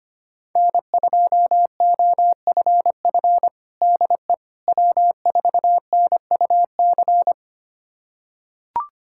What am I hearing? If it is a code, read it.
Morse code, heard as a keyed tone, about 25 wpm, 700 Hz. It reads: N2OFF DE W4NUC
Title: USA Callsign DE Callsign - 5 Characters - 25wpm